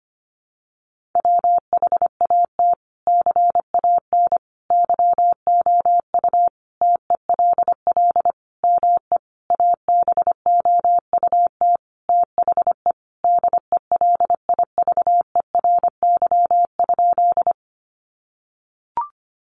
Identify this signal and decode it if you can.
Morse code, heard as a keyed tone, about 25 wpm, 700 Hz. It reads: WHAT CAN YOU TELL ME ABOUT THE DELIVERY?
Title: Sentences from Top 1500 Words 25wpm